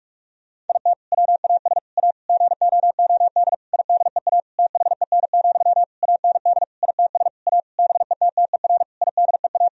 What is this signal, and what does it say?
Morse code, heard as a keyed tone, about 46 wpm, 700 Hz. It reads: IT WAS A GOOD IDEA THEN, AND ITS A BETTER IDEA
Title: Sentences from Top 1000 Words - Review Part 4 - 45wpm